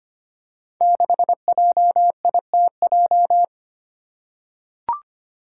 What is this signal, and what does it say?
Morse code, heard as a keyed tone, about 25 wpm, 700 Hz. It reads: 6JITJ
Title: Five Challenging Letter-Number Combinations 25wpm